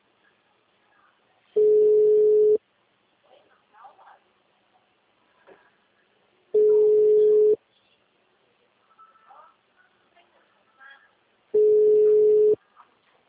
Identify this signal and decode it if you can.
Ringback tone (Europe, 425 Hz), with speech.